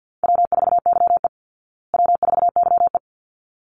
Morse code, heard as a keyed tone, about 50 wpm, 700 Hz. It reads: W4YE W4YE
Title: Call Signs Repeated 3x - 50wpm